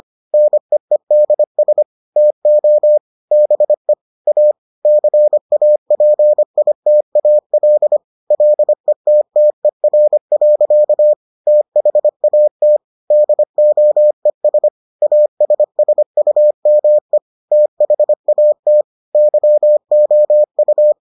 Morse code, heard as a keyed tone, about 25 wpm, 600 Hz. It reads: NEEDS TO BE A CAPITAL LETTER. THAT DOES ASSUME THAT YOU